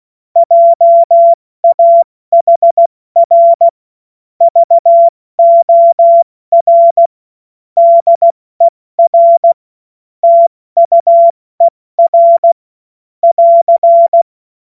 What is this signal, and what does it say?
Morse code, heard as a keyed tone, about 16 wpm, spaced as for 14 wpm, 680 Hz. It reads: JAHR VOR DER TUER <AR>
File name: Morsekurs / Witze / Witz 25. April 70 F80